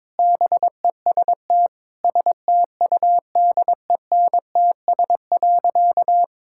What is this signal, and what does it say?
Morse code, heard as a keyed tone, about 21 wpm, 700 Hz. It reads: BEST STUDENTS.
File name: practice / 22wpm / Sentences with 4-Word Phrases - 22wpm